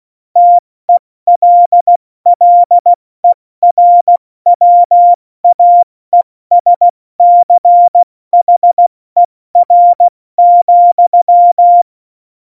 Morse code, heard as a keyed tone, about 16 wpm, 700 Hz. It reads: TELLERWAESCHER,
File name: Morsekurs / Witze / Witz 26. November 2019, 70 F80